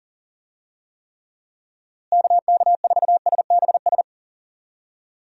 Morse code, heard as a keyed tone, about 40 wpm, 700 Hz. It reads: KK4SBS
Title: USA Call Signs - 6 Characters - Repeated 3x - 40wpm